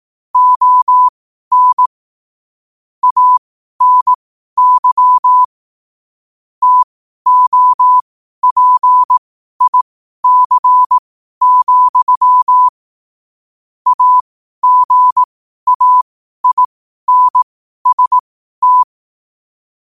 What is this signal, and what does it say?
Morse code, heard as a keyed tone, about 18 wpm, spaced as for 12 wpm, 1000 Hz. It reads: ON ANY TOPIC, AGAINST
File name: qotdfiles / 12wpm